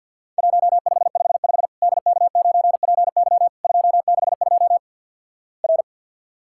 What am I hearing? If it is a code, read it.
Morse code, heard as a keyed tone, about 50 wpm, 700 Hz. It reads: 1555 DK9PY 262 R
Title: Worked All Europe DX Contest - QTC Exchange with Repeats - 50wpm